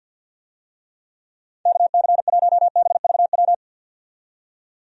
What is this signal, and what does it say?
Morse code, heard as a keyed tone, about 48 wpm, 700 Hz. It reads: KC1BVW
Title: USA Call Signs - 6 Characters - 50wpm